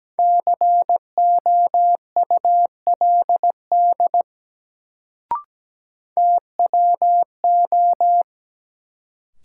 Morse code, heard as a keyed tone, about 17 wpm, 700 Hz. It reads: COULD TWO
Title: Sentences from Top 100 Words - New Word Drill - 17wpm